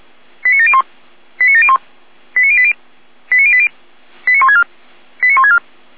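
Five-tone selective call (ZVEI), heard as quick runs of five tones, tones of about 70 ms. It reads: 89701 89701 89080 89080 89164 89164
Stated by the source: ZVEI five-tone call 89700 received by SDR.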